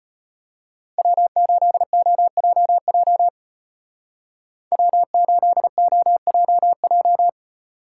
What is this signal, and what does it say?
Morse code, heard as a keyed tone, about 33 wpm, 700 Hz. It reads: W8OJJ W8OJJ